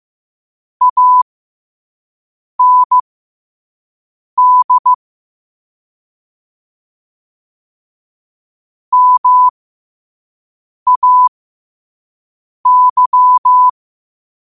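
Morse code, heard as a keyed tone, about 15 wpm, spaced as for 5 wpm, 1000 Hz. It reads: AND MAY
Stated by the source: Morse sent at 5 wpm.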